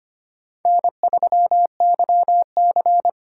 Morse code, heard as a keyed tone, about 25 wpm, 700 Hz. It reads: N3YC
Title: USA Call Signs - 4x Character Spacing - 25wpm